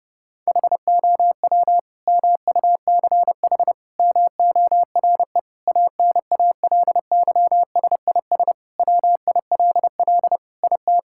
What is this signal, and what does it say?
Morse code, heard as a keyed tone, about 30 wpm, 700 Hz. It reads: HOW MUCH MORE ANALYSIS WILL IT
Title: Sentences from Top 1000 Words - Review Part 5 - 30wpm